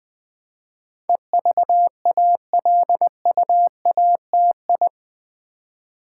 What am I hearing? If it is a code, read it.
Morse code, heard as a keyed tone, about 20 wpm, 700 Hz. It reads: EVALUATI
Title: Build A Word - 4 Letters At A Time - 10-Letter Words - 20wpm